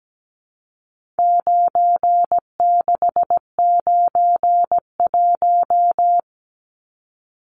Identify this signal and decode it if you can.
Morse code, heard as a keyed tone, about 17 wpm, 700 Hz. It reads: 9691
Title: Numbers - 4-Digit 17wpm